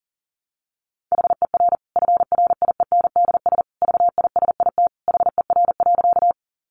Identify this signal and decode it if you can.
Morse code, heard as a keyed tone, about 40 wpm, 700 Hz. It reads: HER FRIENDS VISIT HER.
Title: Easy Phrases 40wpm